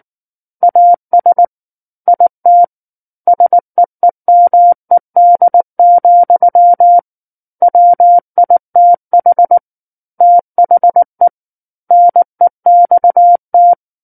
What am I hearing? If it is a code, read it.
Morse code, heard as a keyed tone, about 19 wpm, 700 Hz. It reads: AS IT SEEMED, WITH THE NEXT
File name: Book One / 10-Chapter 10 - 19 WPM